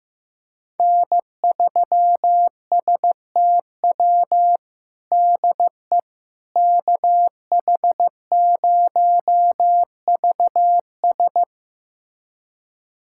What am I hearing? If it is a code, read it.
Morse code, heard as a keyed tone, about 15 wpm, 700 Hz. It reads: N3STW DE KH0VS
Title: USA Callsign DE Callsign - 5 Characters - 15wpm